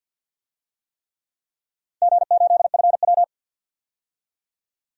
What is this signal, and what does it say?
Morse code, heard as a keyed tone, about 47 wpm, 700 Hz. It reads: G8FW
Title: DXCC-100 Call Signs - 4 Characters - 50wpm